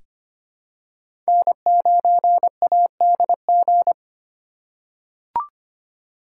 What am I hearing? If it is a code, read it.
Morse code, heard as a keyed tone, about 25 wpm, 700 Hz. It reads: N9ADG
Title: K1USN SST QSO Practice - 25wpm